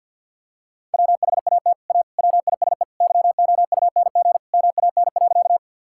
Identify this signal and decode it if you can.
Morse code, heard as a keyed tone, about 50 wpm, 700 Hz. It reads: WHAT A WISE YOUNG MAN.
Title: Sentences from Top 1600 Words 50wpm